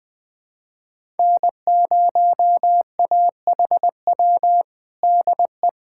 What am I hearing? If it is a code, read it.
Morse code, heard as a keyed tone, about 20 wpm, 700 Hz. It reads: N0AHW DE
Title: USA Callsign DE Callsign - 5 Characters - 20wpm